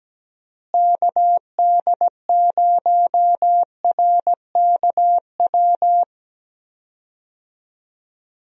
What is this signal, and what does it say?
Morse code, heard as a keyed tone, about 17 wpm, 700 Hz. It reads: KD0RKW
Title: USA Call Signs - 6 Characters - Repeated 3x - 17wpm